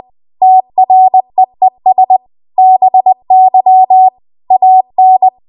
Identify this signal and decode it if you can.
Morse code, heard as a keyed tone, about 20 wpm, 750 Hz. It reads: TREES BY AN
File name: Morse / 20WPM